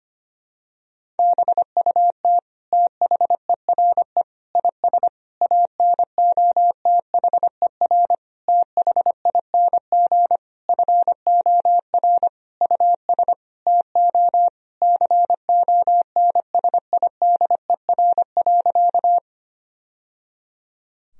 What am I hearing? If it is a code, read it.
Morse code, heard as a keyed tone, about 25 wpm, 700 Hz. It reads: BUT THERE IS ANOTHER THING FOR US TO CONSIDER.
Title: Sentences from Top 1000 Words - Review Part 4 - 25wpm